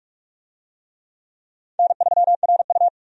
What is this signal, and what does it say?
Morse code, heard as a keyed tone, about 44 wpm, 700 Hz. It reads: N3RU
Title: USA Call Signs - 4 Characters - 45wpm